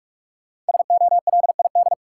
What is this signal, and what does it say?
Morse code, heard as a keyed tone, about 44 wpm, 700 Hz. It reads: SOLID
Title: Sentences from Top 1500 Words - New Word Drill Alt - 45wpm